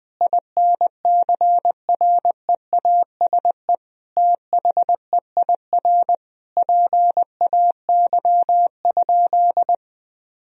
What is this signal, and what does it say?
Morse code, heard as a keyed tone, about 20 wpm, 700 Hz. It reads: INCREASE THEIR PAY?
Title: Sentences from Top 1000 Words - Review Part 4 - 20wpm